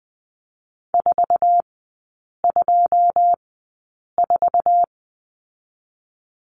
Morse code, heard as a keyed tone, about 20 wpm, spaced as for 8 wpm, 700 Hz. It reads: 424
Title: 081 Lesson 14e - Number 2 - Three Characters 20wpm